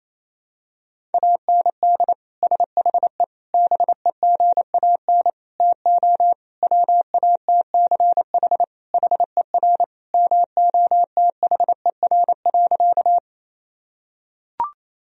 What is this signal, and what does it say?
Morse code, heard as a keyed tone, about 27 wpm, 700 Hz. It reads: AND SHE BEGAN TO WATCH HER MOTHER.
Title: Sentences from Top 1000 Words - Review Part 1 - 28wpm